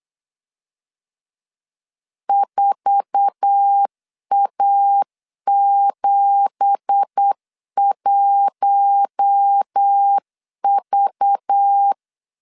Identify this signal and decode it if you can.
Morse code, heard as a keyed tone, about 8 wpm, 800 Hz. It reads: 4A71V
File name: CW / DET / 1 9 1500C S A 8wpm 5637